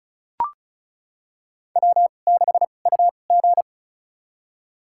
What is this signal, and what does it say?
Morse code, heard as a keyed tone, about 36 wpm, spaced as for 25 wpm, 700 Hz. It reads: W6UG
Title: USA Call Signs - 4 Characters - 2x Character Spacing - 35wpm